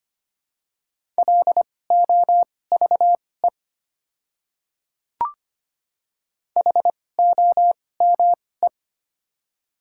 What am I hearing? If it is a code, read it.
Morse code, heard as a keyed tone, about 25 wpm, spaced as for 18 wpm, 700 Hz. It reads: LOVE HOME
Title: 2x Character Spacing - Top 500 Words - 25wpm